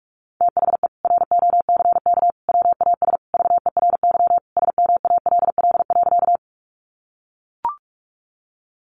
Morse code, heard as a keyed tone, about 44 wpm, 700 Hz. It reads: THE ROCK WAS VERY SMALL.
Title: Sentences from Top 400 Words 45wpm